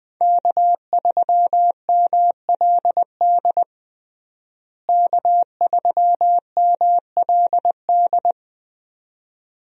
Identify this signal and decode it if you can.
Morse code, heard as a keyed tone, about 21 wpm, 700 Hz. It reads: K3MLD K3MLD
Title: USA Call Signs - 5 Characters - Repeated 2x - 20wpm